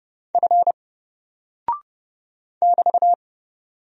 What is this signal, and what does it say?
Morse code, heard as a keyed tone, about 30 wpm, 700 Hz. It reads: F <BT>
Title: Single Character (Letter:Number:Punctuation:Prosign) 30wpm